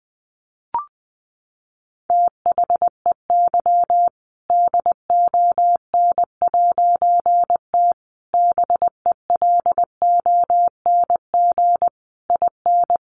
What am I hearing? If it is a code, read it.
Morse code, heard as a keyed tone, about 20 wpm, 700 Hz. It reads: THEY DON'T BELONG IN
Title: A Sound Of Thunder 20wpm